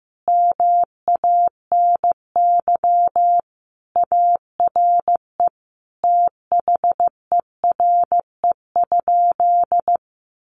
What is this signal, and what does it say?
Morse code, heard as a keyed tone, about 15 wpm, 700 Hz. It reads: MANY ARE THERE?
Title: Sentences from Top 100 Words 15wpm